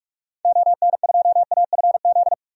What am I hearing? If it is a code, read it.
Morse code, heard as a keyed tone, about 46 wpm, 700 Hz. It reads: ON2AFZ
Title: DXCC-100 Call Signs - 6 Characters - 45wpm